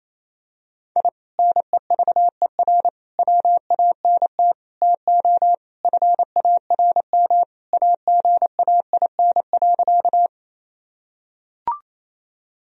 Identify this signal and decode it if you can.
Morse code, heard as a keyed tone, about 27 wpm, 700 Hz. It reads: I NEVER WANT TO FARM AGAIN.
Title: Sentences from Top 1000 Words - Review Part 2 - 28wpm